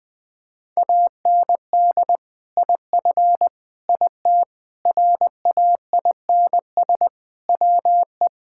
Morse code, heard as a keyed tone, about 20 wpm, 700 Hz. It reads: AND IF IT RAINS WE
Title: Sentences with 2-Word Phrases - 20wpm